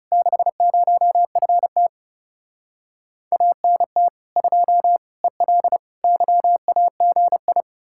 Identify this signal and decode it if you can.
Morse code, heard as a keyed tone, about 40 wpm, 700 Hz. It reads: 60FT ANT 2 EL YAGI